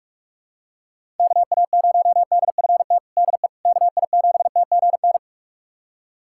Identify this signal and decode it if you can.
Morse code, heard as a keyed tone, about 46 wpm, 700 Hz. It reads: KA0DFT DE KI7TGN
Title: USA Callsign DE Callsign - 6 Characters - 45wpm